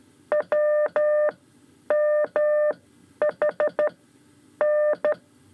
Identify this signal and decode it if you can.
Morse code, heard as a keyed tone, about 13 wpm, spaced as for 9 wpm, 600 Hz. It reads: WMHN
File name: 5 Lezione CW 2